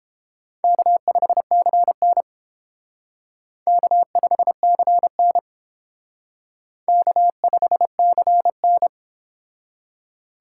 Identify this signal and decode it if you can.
Morse code, heard as a keyed tone, about 28 wpm, 700 Hz. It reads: K5CN K5CN K5CN